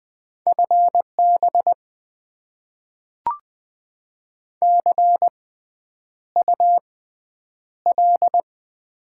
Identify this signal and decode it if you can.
Morse code, heard as a keyed tone, about 20 wpm, spaced as for 9 wpm, 700 Hz. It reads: FB CUL
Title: Top 90 QSO Elements - 6x Character Spacing - 20wpm